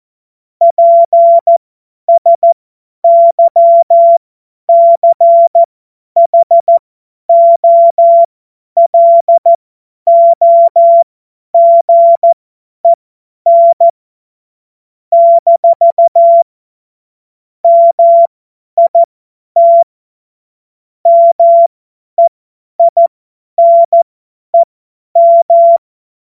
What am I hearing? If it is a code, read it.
Morse code, heard as a keyed tone, about 14 wpm, spaced as for 10 wpm, 680 Hz. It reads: PSYCHOLOGEN - MIT MEINEM
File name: Morsekurs / Witze / Witz 02 April 2024 50 F70